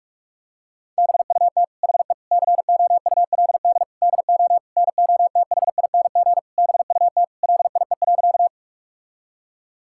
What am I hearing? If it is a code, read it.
Morse code, heard as a keyed tone, about 41 wpm, 700 Hz. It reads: BUT HE COULD DO NOTHING BUT LIE.
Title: Sentences from Top 1600 Words 45wpm